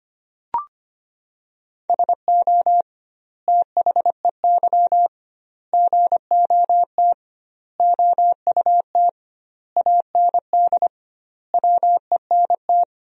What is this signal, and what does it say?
Morse code, heard as a keyed tone, about 25 wpm, 700 Hz. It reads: SO THEY GOT OUT AND WENT
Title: 2x Word-Spacing - Sentences from Top 300 Words - 25wpm